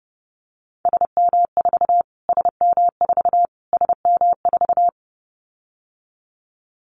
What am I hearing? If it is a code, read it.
Morse code, heard as a keyed tone, about 28 wpm, 700 Hz. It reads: SM4 SM4 SM4
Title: Callsign Prefixes for Top 25 DXCC - Repeated 3x - 30wpm